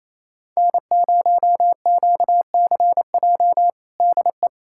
Morse code, heard as a keyed tone, about 28 wpm, 700 Hz. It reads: N0QCJ DE
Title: Practice QSO - Exchange 4 - Ending - 28wpm